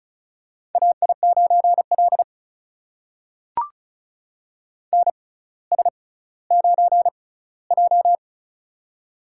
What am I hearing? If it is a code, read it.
Morse code, heard as a keyed tone, about 36 wpm, spaced as for 18 wpm, 700 Hz. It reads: AI9L NS9J
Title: USA Call Signs - 4 Characters - 6x Character Spacing - 35wpm